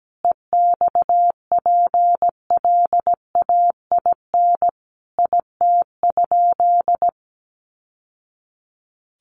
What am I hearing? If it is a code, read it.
Morse code, heard as a keyed tone, about 17 wpm, 700 Hz. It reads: EXPLAIN IT?
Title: Sentences from Top 1100 Words 17wpm